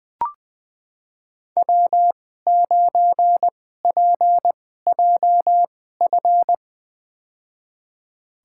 Morse code, heard as a keyed tone, about 20 wpm, spaced as for 15 wpm, 700 Hz. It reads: W9PJF
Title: USA Call Signs - 5 Characters - 2x Character Spacing - 20wpm